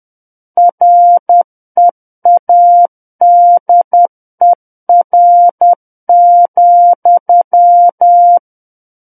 Morse code, heard as a keyed tone, about 10 wpm, 700 Hz. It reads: READER,
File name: Book One / 01-Chapter 1 - 10 WPM